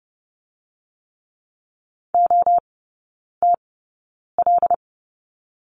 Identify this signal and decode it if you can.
Morse code, heard as a keyed tone, about 30 wpm, spaced as for 9 wpm, 700 Hz. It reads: OTL
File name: Lesson 16e - Letter c - Three Characters 30wpm